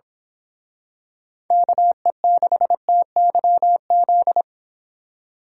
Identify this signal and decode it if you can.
Morse code, heard as a keyed tone, about 26 wpm, 700 Hz. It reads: KE6TYZ
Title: Speed Racing - USA Call Signs 22wpm